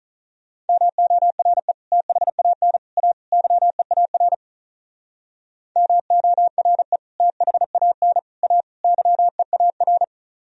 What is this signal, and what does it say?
Morse code, heard as a keyed tone, about 44 wpm, 700 Hz. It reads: MORE THAN A YEAR MORE THAN A YEAR